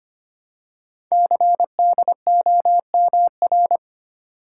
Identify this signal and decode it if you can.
Morse code, heard as a keyed tone, about 25 wpm, 700 Hz. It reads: CDOMR
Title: Code Groups - 5 Letters - 25wpm